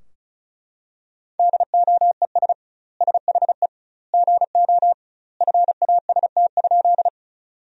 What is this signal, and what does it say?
Morse code, heard as a keyed tone, about 34 wpm, 700 Hz. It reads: DOES SHE GO FAST?
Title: 2x Word-Spacing - Sentences from Top 400 Words - 35wpm